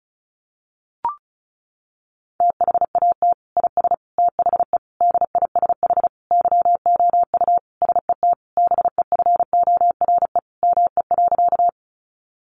Morse code, heard as a keyed tone, about 34 wpm, 700 Hz. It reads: THAT IS THE DISH YOU SET BEFORE ME.
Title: Sentences from Top 1300 Words 35wpm